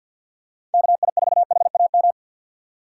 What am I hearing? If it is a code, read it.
Morse code, heard as a keyed tone, about 46 wpm, 700 Hz. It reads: KI4HAM
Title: USA Call Signs - 6 Characters - 3x Character Spacing - 50wpm